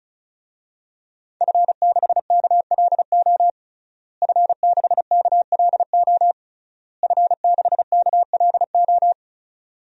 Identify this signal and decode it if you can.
Morse code, heard as a keyed tone, about 36 wpm, 700 Hz. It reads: F6KLO F6KLO F6KLO
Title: DXCC-100 Call Signs - 5 Characters - Repeated 3x - 35wpm